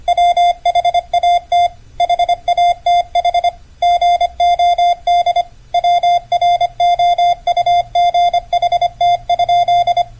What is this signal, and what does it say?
Morse code, heard as a keyed tone, about 25 wpm, 700 Hz. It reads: WHAT HATH GOD WROUGHT?